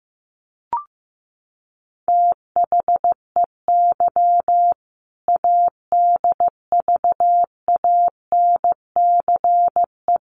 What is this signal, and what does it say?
Morse code, heard as a keyed tone, about 15 wpm, 700 Hz. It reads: THEY ADVANCE
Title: Sentences from Top 1400 Words 15wpm